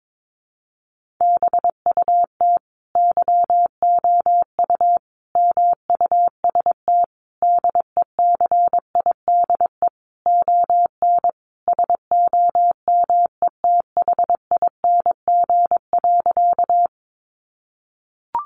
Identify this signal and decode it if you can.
Morse code, heard as a keyed tone, about 22 wpm, 700 Hz. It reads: BUT YOU MUST DECIDE ON SOMETHING.
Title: Sentences from Top 600 Words 22wpm